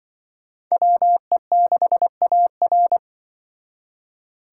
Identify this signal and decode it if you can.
Morse code, heard as a keyed tone, about 24 wpm, 700 Hz. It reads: WE6AR
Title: Speed Racing - USA 2x2 Call Signs 20wpm